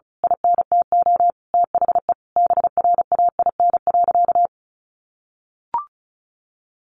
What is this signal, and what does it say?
Morse code, heard as a keyed tone, about 34 wpm, 700 Hz. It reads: INTO THE BRAIN.
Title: A Sound Of Thunder 35wpm